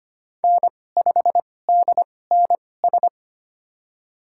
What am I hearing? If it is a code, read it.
Morse code, heard as a keyed tone, about 25 wpm, spaced as for 18 wpm, 700 Hz. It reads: N5DNS
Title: USA Call Signs - 5 Characters - 2x Character Spacing - 25wpm